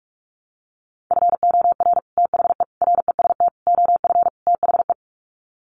Morse code, heard as a keyed tone, about 44 wpm, 700 Hz. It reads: FOR THE REST OF THE
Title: Sets of 5 Words 45wpm